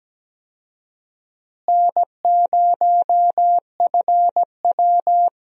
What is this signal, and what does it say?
Morse code, heard as a keyed tone, about 17 wpm, 700 Hz. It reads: N0FW
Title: USA Call Signs - 4 Characters - Repeated 2x - 17wpm